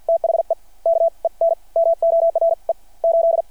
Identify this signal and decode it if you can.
Morse code, heard as a keyed tone, about 48 wpm, spaced as for 38 wpm, 660 Hz. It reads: THE KENMORE 8
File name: archived / CRSnet